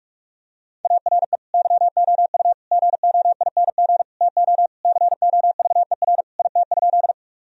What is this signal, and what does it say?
Morse code, heard as a keyed tone, about 42 wpm, 700 Hz. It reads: ARE YOU GOING TO COVER IT?
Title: Sentences from Top 1000 Words - Review Part 1 - 45wpm